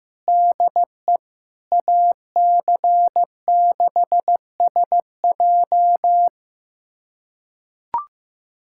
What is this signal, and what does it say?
Morse code, heard as a keyed tone, about 15 wpm, 700 Hz. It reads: DE AC6SJ
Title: USA Callsign DE Callsign - 5 Characters - 15wpm